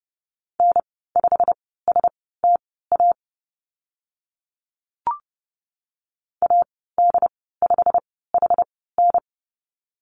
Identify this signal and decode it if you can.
Morse code, heard as a keyed tone, about 30 wpm, spaced as for 17 wpm, 690 Hz. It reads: N5STA AD5HN
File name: Lesson 15g - Number 5 - Call Signs 30wpm